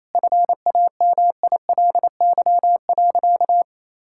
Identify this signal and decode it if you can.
Morse code, heard as a keyed tone, about 27 wpm, 700 Hz. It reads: FAMILY.
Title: Sentences from Top 1000 Words - Review Part 6 - 28wpm